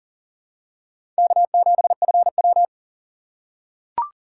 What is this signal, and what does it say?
Morse code, heard as a keyed tone, about 40 wpm, 700 Hz. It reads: K7FW
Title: USA Call Signs - 4 Characters - 2x Character Spacing - 40wpm